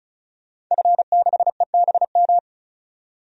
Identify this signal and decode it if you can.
Morse code, heard as a keyed tone, about 36 wpm, 700 Hz. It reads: F6EBM
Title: DXCC-100 Call Signs - 5 Characters - 35wpm